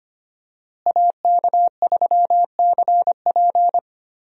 Morse code, heard as a keyed tone, about 25 wpm, 700 Hz. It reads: AK3CP
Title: USA Call Signs - 5 Characters - Repeated 2x - 25wpm